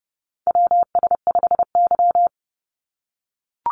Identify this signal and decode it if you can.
Morse code, heard as a keyed tone, about 30 wpm, 700 Hz. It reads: WS5Y
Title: USA Call Signs 30wpm